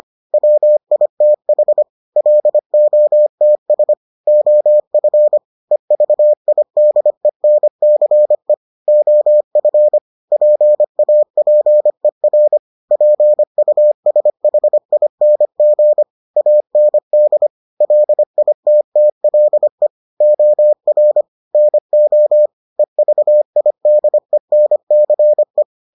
Morse code, heard as a keyed tone, about 25 wpm, 600 Hz. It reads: WITH LOTS OF EVIDENCE OF PAPER PUSHING AND LITTLE OR NO EVIDENCE